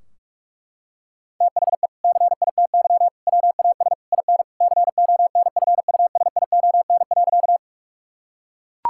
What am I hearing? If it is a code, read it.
Morse code, heard as a keyed tone, about 41 wpm, 700 Hz. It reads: THE CITY WAS IN CONFUSION.
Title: Sentences from Top 1600 Words 45wpm